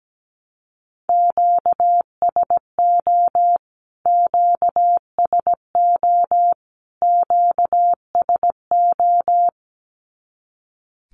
Morse code, heard as a keyed tone, about 17 wpm, 700 Hz. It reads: QSO QSO QSO
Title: Q-Codes Repeated 3x 17wpm